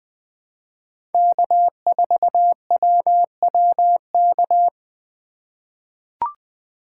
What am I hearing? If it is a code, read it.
Morse code, heard as a keyed tone, about 20 wpm, 700 Hz. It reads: K4WWK
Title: USA Call Signs - 5 Characters - 3x Character Spacing - 20wpm